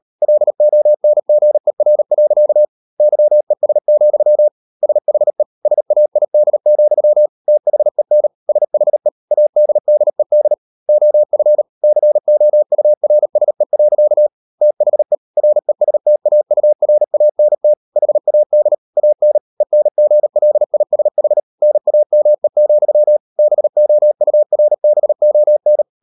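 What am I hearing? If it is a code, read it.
Morse code, heard as a keyed tone, about 38 wpm, 600 Hz. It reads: LONGER. YES, SHE SAID, THEN SHE ADDED OF COURSE. THE RESTAURANT HAD AN ENGLISH NAME, BOURBON
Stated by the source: Morse code practice at 38 WPM, sending Ken Follett's novel Never in English, from CW med Gnister.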